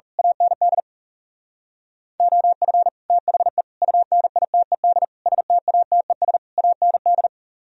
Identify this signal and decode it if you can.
Morse code, heard as a keyed tone, about 34 wpm, 700 Hz. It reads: AND OF THE UNITED STATES AND
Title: Speed-Racing - Sets of 5 Words 30wpm